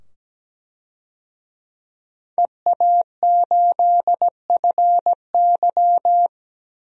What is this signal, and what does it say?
Morse code, heard as a keyed tone, about 17 wpm, 700 Hz. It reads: EA8FY
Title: DXCC-100 Call Signs - 5 Characters - 17wpm